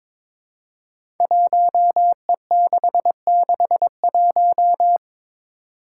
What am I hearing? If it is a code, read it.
Morse code, heard as a keyed tone, about 22 wpm, 700 Hz. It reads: 1E661
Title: Five Challenging Letter-Number Combinations 22wpm